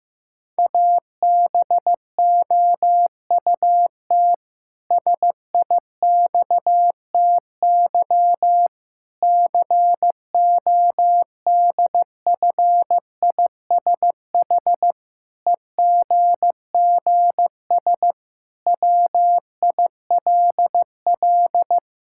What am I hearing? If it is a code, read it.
Morse code, heard as a keyed tone, about 15 wpm, 700 Hz. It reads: ABOUT SIXTY CODFISH EGGS WILL
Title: Pangram Challenge v2 - 15wpm